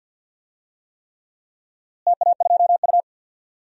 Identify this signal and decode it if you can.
Morse code, heard as a keyed tone, about 48 wpm, 700 Hz. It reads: TA2U
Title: DXCC-100 Call Signs - 4 Characters - Repeated 3x - 50wpm